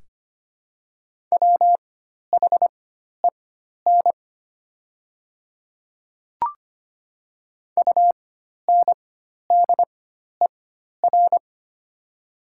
Morse code, heard as a keyed tone, about 25 wpm, spaced as for 12 wpm, 690 Hz. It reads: WHEN UNDER
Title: 4x Character Spacing - Top 200 Words - 25wpm